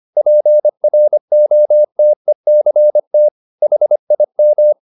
Morse code, heard as a keyed tone, about 25 wpm, 600 Hz. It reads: PROTECT HIM